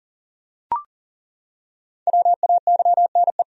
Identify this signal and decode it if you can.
Morse code, heard as a keyed tone, about 37 wpm, 700 Hz. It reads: WAYNE
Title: USA Common Names - Max 5 Letters - 40wpm